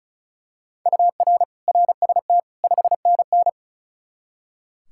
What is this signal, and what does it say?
Morse code, heard as a keyed tone, about 36 wpm, 700 Hz. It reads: UR RST 5NN
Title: K1USN SST QSO Practice 35wpm